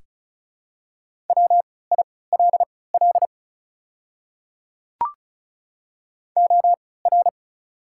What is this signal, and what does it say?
Morse code, heard as a keyed tone, about 34 wpm, spaced as for 20 wpm, 690 Hz. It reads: WILL OR
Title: 3x Character Spacing - Top 100 Words - 35wpm